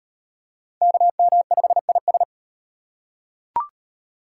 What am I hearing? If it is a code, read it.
Morse code, heard as a keyed tone, about 35 wpm, 700 Hz. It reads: KM5IS